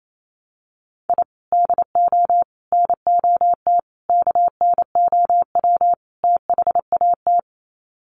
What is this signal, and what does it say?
Morse code, heard as a keyed tone, about 28 wpm, 700 Hz. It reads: I DO NOT KNOW THAT
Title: Sets of 5 Words 28wpm